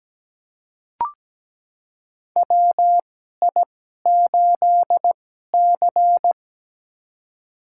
Morse code, heard as a keyed tone, about 17 wpm, spaced as for 12 wpm, 700 Hz. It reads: WI8C
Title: USA Call Signs - 4 Characters - 2x Character Spacing - 17wpm